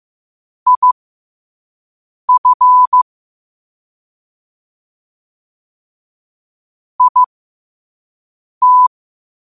Morse code, heard as a keyed tone, about 15 wpm, spaced as for 5 wpm, 1000 Hz. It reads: IF IT
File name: qotdfiles / 5wpm